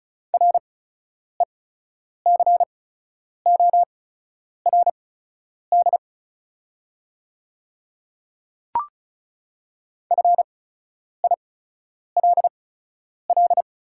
Morse code, heard as a keyed tone, about 36 wpm, spaced as for 10 wpm, 690 Hz. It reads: RECORD FILL
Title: 8x Character Spacing - Top 500 Words - 35wpm